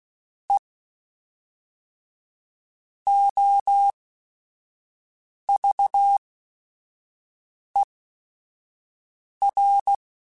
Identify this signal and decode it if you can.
Morse code, heard as a keyed tone, about 16 wpm, spaced as for 5 wpm, 800 Hz. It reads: E OVER